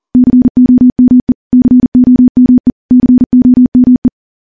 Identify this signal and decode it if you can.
Morse code, heard as a keyed tone, about 40 wpm, 250 Hz. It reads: COME COME COME
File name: morse / morse-training-40WPM